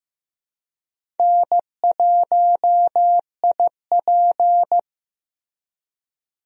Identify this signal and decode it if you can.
Morse code, heard as a keyed tone, about 15 wpm, 700 Hz. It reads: N1IP
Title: USA Call Signs - 4 Characters - Repeated 3x - 15wpm